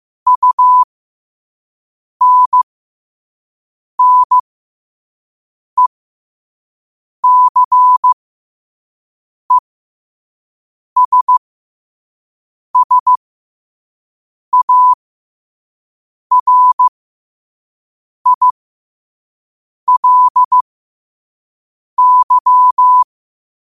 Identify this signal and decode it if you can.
Morse code, heard as a keyed tone, about 15 wpm, spaced as for 5 wpm, 1000 Hz. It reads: UNNECESSARILY
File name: qotdfiles / 5wpm